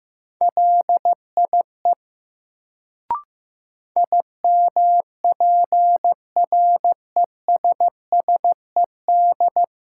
Morse code, heard as a keyed tone, about 15 wpm, 700 Hz. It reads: LIE IMPRESSED
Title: Sentences from Top 1600 Words - New Word Drill Alt - 15wpm